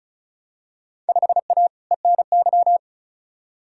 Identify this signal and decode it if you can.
Morse code, heard as a keyed tone, about 34 wpm, 700 Hz. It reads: 5A ENY
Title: Field Day Exchanges - 35wpm